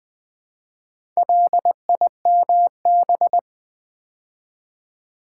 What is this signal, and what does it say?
Morse code, heard as a keyed tone, about 20 wpm, 700 Hz. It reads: LIMB
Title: Build A Word - 2 Letters At A Time - 4-Letter Words - 20wpm